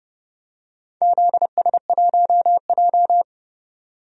Morse code, heard as a keyed tone, about 31 wpm, 700 Hz. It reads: ZS1J
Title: DXCC-100 Call Signs - 4 Characters - 30wpm